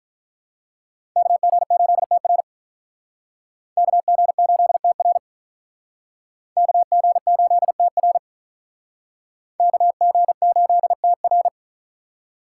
Morse code, heard as a keyed tone, about 46 wpm, 700 Hz. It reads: KG8TR KG8TR KG8TR KG8TR